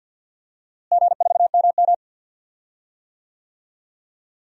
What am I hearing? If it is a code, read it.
Morse code, heard as a keyed tone, about 53 wpm, 700 Hz. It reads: G4MM
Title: DXCC-100 Call Signs - 4 Characters - Repeated 2x - 50wpm